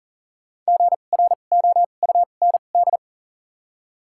Morse code, heard as a keyed tone, about 40 wpm, spaced as for 29 wpm, 700 Hz. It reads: GROUND
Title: 2x Character Spacing - Top 500 Words - 40wpm